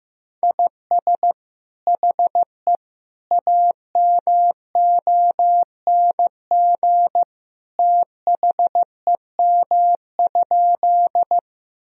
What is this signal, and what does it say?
Morse code, heard as a keyed tone, about 15 wpm, 700 Hz. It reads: IS HE AMONG THEM?
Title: Sentences from Top 1000 Words - Review Part 3 - 15wpm